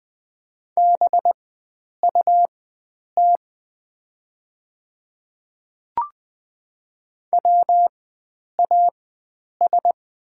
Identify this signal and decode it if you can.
Morse code, heard as a keyed tone, about 20 wpm, spaced as for 9 wpm, 700 Hz. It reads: BUT WAS
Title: 4x Character Spacing - Top 100 Words - 20wpm